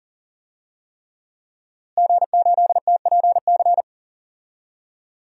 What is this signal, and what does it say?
Morse code, heard as a keyed tone, about 40 wpm, 700 Hz. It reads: G8TPC
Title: DXCC-100 Call Signs - 5 Characters - Repeated 3x - 40wpm